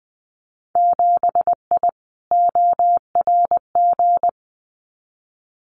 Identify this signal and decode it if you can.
Morse code, heard as a keyed tone, about 20 wpm, 700 Hz. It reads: 7I ORG
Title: Winter Field Day Exchanges Encore 20wpm